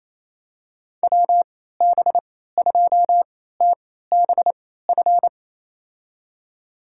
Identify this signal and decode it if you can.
Morse code, heard as a keyed tone, about 27 wpm, spaced as for 16 wpm, 700 Hz. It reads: WB2TBF
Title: USA Call Signs - 3x Character Spacing - 28wpm